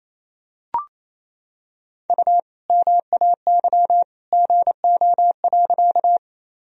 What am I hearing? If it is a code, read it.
Morse code, heard as a keyed tone, about 28 wpm, 700 Hz. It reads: U MAY GO.
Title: Sentences from Top 100 Words and Abbreviations 28wpm